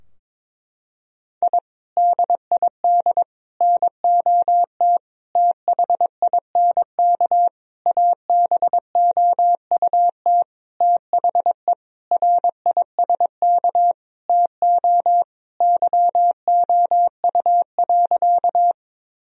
Morse code, heard as a keyed tone, about 22 wpm, 700 Hz. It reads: I DID NOT THINK ABOUT THE RISK TO YOU.
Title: Sentences from Top 1000 Words - Review Part 4 - 22wpm